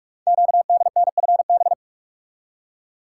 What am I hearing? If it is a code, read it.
Morse code, heard as a keyed tone, about 44 wpm, 700 Hz. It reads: QDNFB
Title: Code Groups - 5 Letters - 45wpm